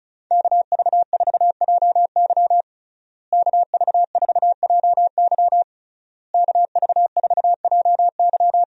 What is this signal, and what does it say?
Morse code, heard as a keyed tone, about 35 wpm, 700 Hz. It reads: KV4JY KV4JY KV4JY
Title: USA Call Signs - 5 Characters - Repeated 3x - 35wpm